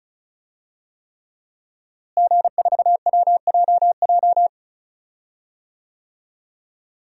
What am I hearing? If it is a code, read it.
Morse code, heard as a keyed tone, about 36 wpm, 700 Hz. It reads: G4WJJ
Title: DXCC-100 Call Signs - 5 Characters - Repeated 3x - 35wpm